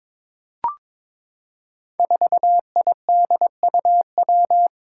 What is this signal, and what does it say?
Morse code, heard as a keyed tone, about 22 wpm, 700 Hz. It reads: 4IDUW
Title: Five Challenging Letter-Number Combinations 22wpm